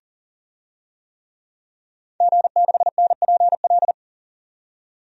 Morse code, heard as a keyed tone, about 40 wpm, 700 Hz. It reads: G6NPL
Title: DXCC-100 Call Signs - 5 Characters - 40wpm